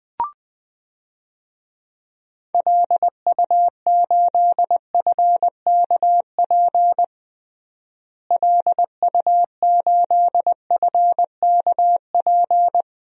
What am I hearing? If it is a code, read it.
Morse code, heard as a keyed tone, about 20 wpm, 700 Hz. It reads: LU8FKP LU8FKP
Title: DXCC-100 Call Signs - 6 Characters - Repeated 3x - 20wpm